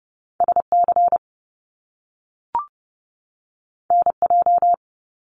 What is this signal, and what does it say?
Morse code, heard as a keyed tone, about 30 wpm, 700 Hz. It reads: SC NJ
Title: US States 30wpm